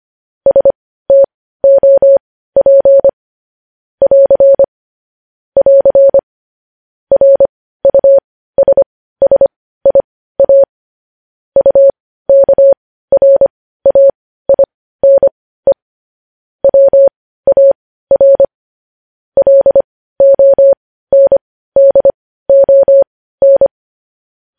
Morse code, heard as a keyed tone, about 25 wpm, spaced as for 15 wpm, 550 Hz. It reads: STOP <AR> <AR> RUSSIA UKRAINE WAR LONDON
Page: News Headlines in Morse Code at 15 WPM